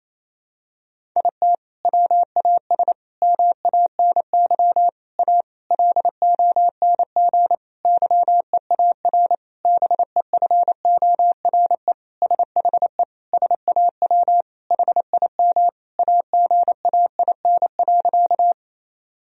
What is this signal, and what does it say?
Morse code, heard as a keyed tone, about 28 wpm, 700 Hz. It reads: IT WAS MANY A LONG YEAR BEFORE SHE SAW HIM AGAIN.
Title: Sentences from Top 1000 Words - Review Part 1 - 28wpm